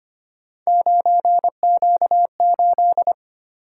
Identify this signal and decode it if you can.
Morse code, heard as a keyed tone, about 25 wpm, 700 Hz. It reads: 9Q8
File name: Three Letter-Number Combinations - 25wpm